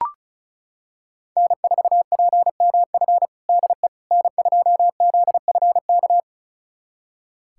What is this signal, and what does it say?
Morse code, heard as a keyed tone, about 36 wpm, 700 Hz. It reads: N4PMF DE N2ZFK
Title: USA Callsign DE Callsign - 5 Characters - 35wpm